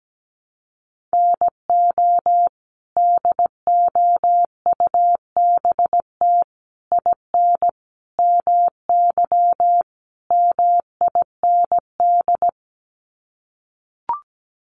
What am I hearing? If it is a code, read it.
Morse code, heard as a keyed tone, about 17 wpm, 700 Hz. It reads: NO DOUBT IN MY MIND
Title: Sets of 5 Words Encore2 17wpm